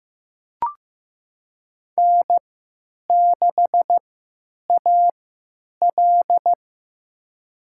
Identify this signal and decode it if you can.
Morse code, heard as a keyed tone, about 15 wpm, spaced as for 9 wpm, 700 Hz. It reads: N6AL
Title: USA Call Signs - 4 Characters - 3x Character Spacing - 15wpm